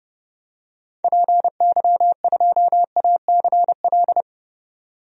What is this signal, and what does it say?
Morse code, heard as a keyed tone, about 31 wpm, 700 Hz. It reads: PY2ACL
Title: DXCC-100 Call Signs - 6 Characters - 30wpm